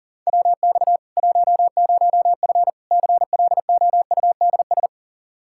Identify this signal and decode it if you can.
Morse code, heard as a keyed tone, about 40 wpm, 700 Hz. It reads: WX 10F CLOUDS
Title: Practice QSO - Exchange 2 - Info - 40wpm